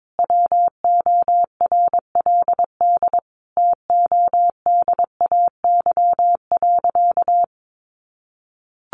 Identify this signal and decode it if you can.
Morse code, heard as a keyed tone, about 22 wpm, 700 Hz. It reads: WORLD TODAY.
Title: Sentences from Top 600 Words 22wpm